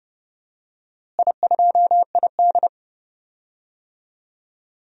Morse code, heard as a keyed tone, about 29 wpm, 700 Hz. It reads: I2ID
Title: DXCC-100 Call Signs - 4 Characters - Repeated 2x - 30wpm